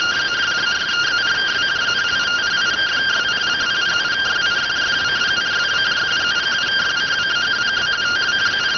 Two-tone teleprinter signal, about 170 Hz shift.